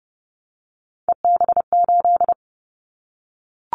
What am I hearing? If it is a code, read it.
Morse code, heard as a keyed tone, about 30 wpm, 700 Hz. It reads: EB8
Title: Callsign Prefixes for Top 50 DXCC - 30wpm